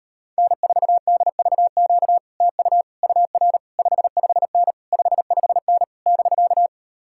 Morse code, heard as a keyed tone, about 39 wpm, 700 Hz. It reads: N4DVQ TU UR 55N 55N <BK>